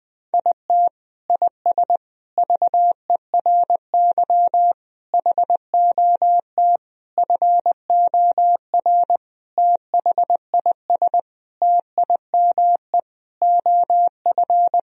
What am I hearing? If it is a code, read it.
Morse code, heard as a keyed tone, about 21 wpm, 700 Hz. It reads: IT IS VERY HOT FOR THIS TIME OF
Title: Sentences from Top 1000 Words - Review Part 3 - 20wpm